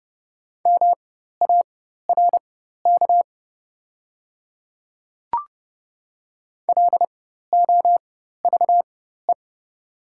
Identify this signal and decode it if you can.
Morse code, heard as a keyed tone, about 30 wpm, spaced as for 14 wpm, 700 Hz. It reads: MARK LOVE
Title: 4x Character Spacing - Top 500 Words - 30wpm